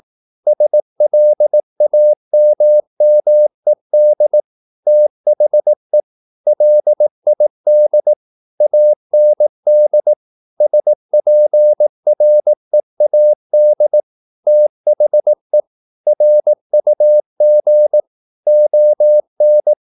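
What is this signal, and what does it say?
Morse code, heard as a keyed tone, about 18 wpm, 600 Hz. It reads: SLAMMED THE LID AND SPREAD THE RUG ON
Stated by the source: Morse practice at 18 wpm, English text, from CW med Gnister.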